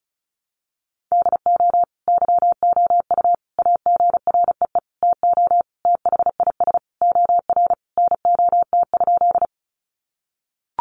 Morse code, heard as a keyed tone, about 34 wpm, 700 Hz. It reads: DO YOU AGREE TO THIS OR NOT?
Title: Sentences from Top 1100 Words 35wpm